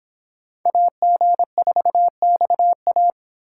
Morse code, heard as a keyed tone, about 24 wpm, 700 Hz. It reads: AG4XA